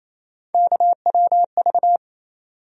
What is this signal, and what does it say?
Morse code, heard as a keyed tone, about 28 wpm, 700 Hz. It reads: KWV
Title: USA Call Sign Postfixes - 3 Letters 28wpm